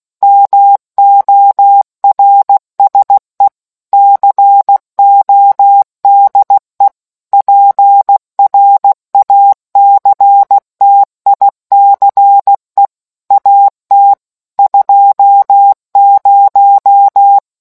Morse code, heard as a keyed tone, about 16 wpm, 800 Hz. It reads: MORSE CODE PRACTICE AT 20